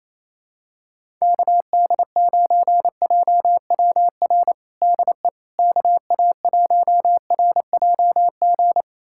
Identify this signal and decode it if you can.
Morse code, heard as a keyed tone, about 28 wpm, 700 Hz. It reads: KD9JWR DE KA1RJG
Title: USA Callsign DE Callsign 28wpm